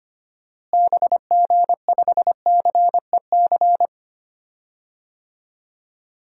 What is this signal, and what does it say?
Morse code, heard as a keyed tone, about 25 wpm, 690 Hz. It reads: BG5CEC
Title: DXCC-100 Call Signs - 6 Characters - Repeated 3x - 25wpm